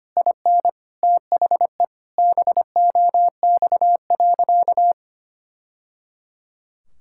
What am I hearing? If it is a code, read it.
Morse code, heard as a keyed tone, about 25 wpm, 700 Hz. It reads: IN THE BOX.
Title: Sentences from Top 1000 Words - Review Part 5 - 25wpm